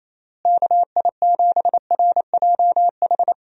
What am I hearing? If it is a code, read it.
Morse code, heard as a keyed tone, about 28 wpm, 700 Hz. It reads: KI7RJH
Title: USA Call Signs - 6 Characters - Repeated 3x - 28wpm